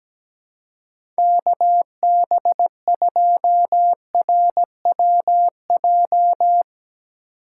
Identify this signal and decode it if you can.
Morse code, heard as a keyed tone, about 17 wpm, 700 Hz. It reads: KB2RWJ
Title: USA Call Signs - 6 Characters - 3x Character Spacing - 17wpm